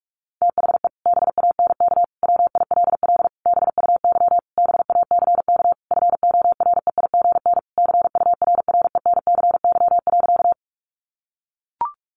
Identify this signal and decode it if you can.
Morse code, heard as a keyed tone, about 44 wpm, 700 Hz. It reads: THE BANK WILL BUY BACK FOREIGN CURRENCY.
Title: Sentences from Top 1500 Words 45wpm